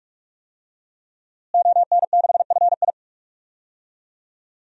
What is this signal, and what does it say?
Morse code, heard as a keyed tone, about 46 wpm, 700 Hz. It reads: ON6FI
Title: DXCC-100 Call Signs - 5 Characters - Repeated 3x - 45wpm